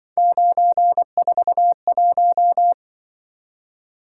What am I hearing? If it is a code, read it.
Morse code, heard as a keyed tone, about 24 wpm, 700 Hz. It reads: 941
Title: Speed-Racing - Three Letter-Number Combinations - 20wpm